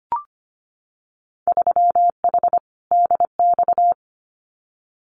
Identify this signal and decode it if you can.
Morse code, heard as a keyed tone, about 25 wpm, 700 Hz. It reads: 3H DX
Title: Winter Field Day Exchanges Encore 25wpm